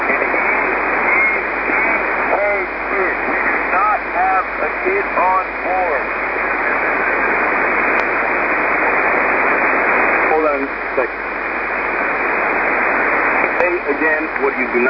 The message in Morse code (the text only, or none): none